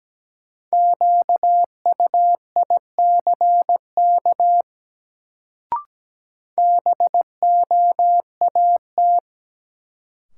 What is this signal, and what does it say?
Morse code, heard as a keyed tone, about 17 wpm, 700 Hz. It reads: QUICK BOAT
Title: Sentences from Top 500 Words - New Word Drill - 17wpm